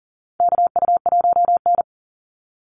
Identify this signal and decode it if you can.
Morse code, heard as a keyed tone, about 40 wpm, 700 Hz. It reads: KU1N
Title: Call Signs Repeated 3x - 40wpm